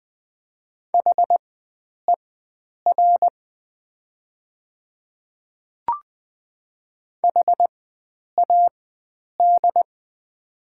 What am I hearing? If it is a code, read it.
Morse code, heard as a keyed tone, about 20 wpm, spaced as for 9 wpm, 690 Hz. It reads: HER HAD
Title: 4x Character Spacing - Top 100 Words - 20wpm